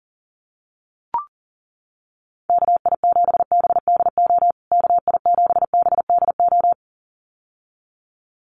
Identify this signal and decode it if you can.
Morse code, heard as a keyed tone, about 40 wpm, 700 Hz. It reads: KI7BDO KI7BDO
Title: USA Call Signs Repeated 2x 40wpm